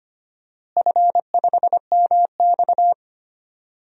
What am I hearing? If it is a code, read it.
Morse code, heard as a keyed tone, about 25 wpm, 700 Hz. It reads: F5MX
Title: DXCC-100 Call Signs - 4 Characters - Repeated 3x - 25wpm